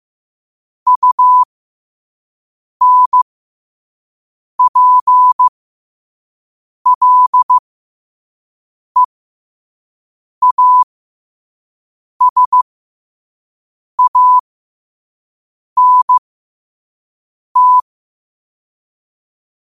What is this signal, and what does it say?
Morse code, heard as a keyed tone, about 15 wpm, spaced as for 5 wpm, 1000 Hz. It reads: UNPLEASANT
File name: qotdfiles / 5wpm